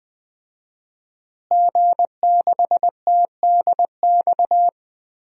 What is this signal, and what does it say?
Morse code, heard as a keyed tone, about 20 wpm, 700 Hz. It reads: G6TDX
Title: DXCC-100 Call Signs - 5 Characters - Repeated 2x - 20wpm